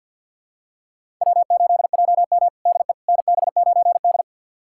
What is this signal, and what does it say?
Morse code, heard as a keyed tone, about 52 wpm, 700 Hz. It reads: W8JM DE NB9D
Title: USA Callsign DE Callsign - 4 Characters - 50wpm